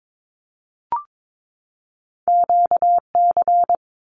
Morse code, heard as a keyed tone, about 22 wpm, 700 Hz. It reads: QC
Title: Canadian Provinces and Territories 22wpm